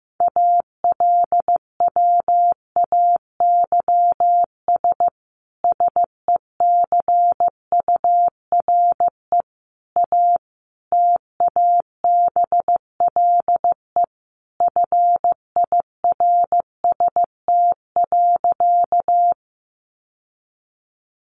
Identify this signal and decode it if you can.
Morse code, heard as a keyed tone, about 15 wpm, 700 Hz. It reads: ALWAYS SECURE A TABLE FIRST.
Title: Sentences from Top 1200 Words 15wpm